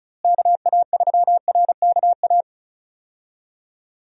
Morse code, heard as a keyed tone, about 36 wpm, 700 Hz. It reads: KA3RKA
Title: USA Call Signs - 6 Characters - Repeated 2x - 35wpm